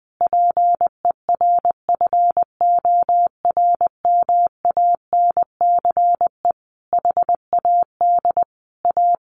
Morse code, heard as a keyed tone, about 20 wpm, 700 Hz. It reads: PERFORMANCE HAD A
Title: Sentences from Top 1400 Words 20wpm